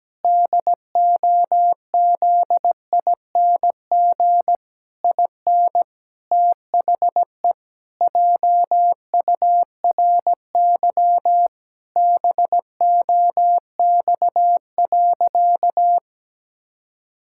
Morse code, heard as a keyed tone, about 17 wpm, 700 Hz. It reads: DOZING IN THE JURY BOX.
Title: Pangram Challenge v2 - 17wpm